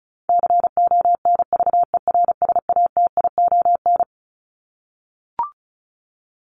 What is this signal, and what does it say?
Morse code, heard as a keyed tone, about 34 wpm, 700 Hz. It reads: CONVERSATION
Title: Top 1751-2000 Words 35wpm